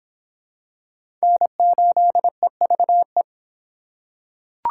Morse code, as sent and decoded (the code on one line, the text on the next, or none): -. ---.. ...- .
N8EVE